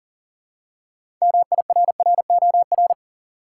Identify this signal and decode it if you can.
Morse code, heard as a keyed tone, about 40 wpm, 700 Hz. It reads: MIRROR
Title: Six Letter Words - 40wpm